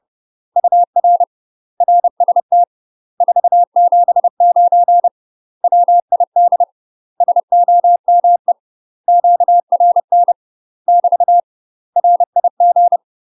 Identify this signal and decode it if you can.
Morse code, heard as a keyed tone, about 30 wpm, 700 Hz. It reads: UR RST 479 WID SOME QRN <BT> RIG
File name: UK MASTER QSOs TRAINING / 30WPM / UK MASTER QSOs 0044